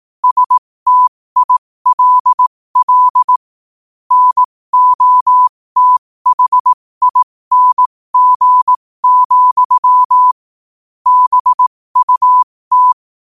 Morse code, heard as a keyed tone, about 18 wpm, spaced as for 15 wpm, 1000 Hz. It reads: STILL NOTHING, BUT
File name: qotdfiles / 15wpm